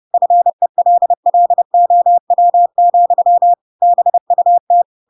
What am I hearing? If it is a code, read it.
Morse code, heard as a keyed tone, about 30 wpm, 700 Hz. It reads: FELLOW, BUT
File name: Pinocchio / Pinocchio ch3 30 30 ik1hge